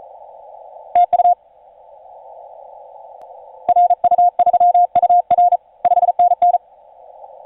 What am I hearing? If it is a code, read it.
Morse code, heard as a keyed tone, about 39 wpm, 710 Hz, with radio noise. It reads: TU RU3UR 5NN
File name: 5u5r 17m cw